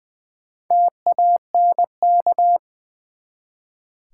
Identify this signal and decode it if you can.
Morse code, heard as a keyed tone, about 20 wpm, 700 Hz. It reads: TANK
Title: Four Letter Words - 20wpm